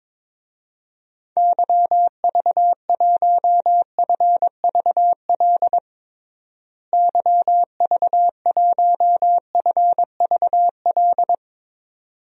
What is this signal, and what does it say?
Morse code, heard as a keyed tone, about 23 wpm, 700 Hz. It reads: YV1FVL YV1FVL
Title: DXCC-100 Call Signs - 6 Characters - Repeated 2x - 22wpm